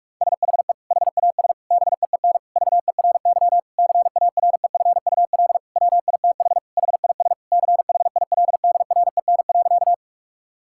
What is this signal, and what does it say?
Morse code, heard as a keyed tone, about 46 wpm, 700 Hz. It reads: SHE HAS BEEN VERY CAREFUL WITH HIS CHILDREN.
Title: Sentences from Top 1600 Words 45wpm